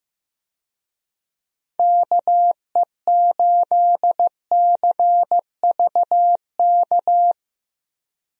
Morse code, heard as a keyed tone, about 15 wpm, 700 Hz. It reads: KE8CVK
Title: USA Call Signs - 6 Characters - Repeated 2x - 15wpm